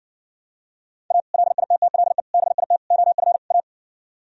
Morse code, heard as a keyed tone, about 59 wpm, 700 Hz. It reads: A LITTLE BIT OF A